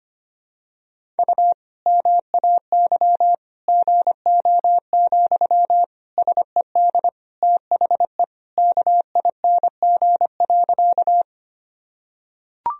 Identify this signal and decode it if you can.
Morse code, heard as a keyed tone, about 25 wpm, 700 Hz. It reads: U MAY GO, SED THE KING.
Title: Sentences from Top 400 Words and Abbreviations 25wpm